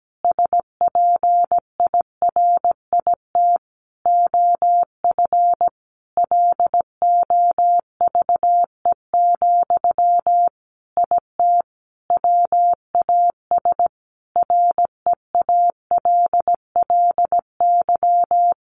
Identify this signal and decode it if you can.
Morse code, heard as a keyed tone, about 17 wpm, 700 Hz. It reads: SPIRIT OF LOVE, IT WAS REALLY
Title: Sentences from Top 1400 Words 17wpm